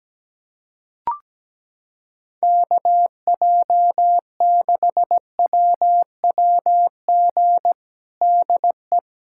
Morse code, heard as a keyed tone, about 17 wpm, 700 Hz. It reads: KJ6WWG DE
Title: USA Callsign DE Callsign - 6 Characters - 17wpm